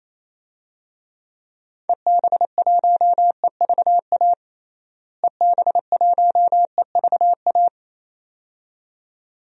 Morse code, heard as a keyed tone, about 28 wpm, 700 Hz. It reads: EB1EVA EB1EVA
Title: DXCC-100 Call Signs - 6 Characters - Repeated 2x - 28wpm